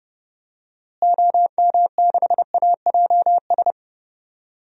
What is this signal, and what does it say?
Morse code, heard as a keyed tone, about 32 wpm, 700 Hz. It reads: OM6AJS
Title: DXCC-100 Call Signs - 6 Characters - Repeated 2x - 30wpm